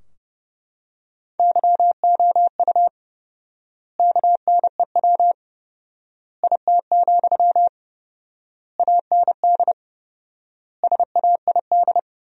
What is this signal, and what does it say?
Morse code, heard as a keyed tone, about 30 wpm, 700 Hz. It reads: YOU KNEW IT, AND SAID
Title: 4x Word-Spacing - Sentences from Top 500 Words - 30wpm